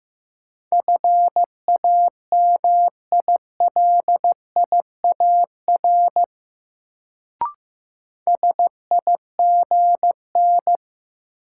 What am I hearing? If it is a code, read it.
Morse code, heard as a keyed tone, about 15 wpm, 700 Hz. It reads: FAMILIAR SIGN
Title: Sentences from Top 1000 Words - New Word Drill - 15wpm